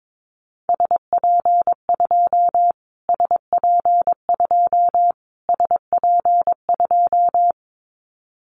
Morse code, heard as a keyed tone, about 22 wpm, 700 Hz. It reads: SP2 SP2 SP2
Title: Callsign Prefixes for Top 25 DXCC - Repeated 3x - 22wpm